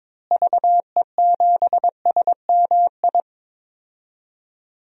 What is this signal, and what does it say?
Morse code, heard as a keyed tone, about 22 wpm, 700 Hz. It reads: VE7SMI
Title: DXCC-100 Call Signs - 6 Characters - Repeated 2x - 22wpm